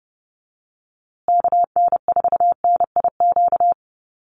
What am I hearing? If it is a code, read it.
Morse code, heard as a keyed tone, about 30 wpm, 700 Hz. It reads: KN4NIQ
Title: USA Call Signs Repeated 3x 30wpm